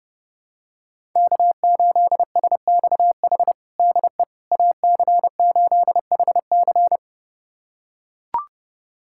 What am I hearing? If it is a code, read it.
Morse code, heard as a keyed tone, about 32 wpm, 700 Hz. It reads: K8SXH DE AC8HC
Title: USA Callsign DE Callsign 30wpm